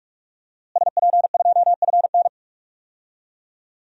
Morse code, heard as a keyed tone, about 47 wpm, 700 Hz. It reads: SP2FN
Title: DXCC-100 Call Signs - 5 Characters - Repeated 2x - 45wpm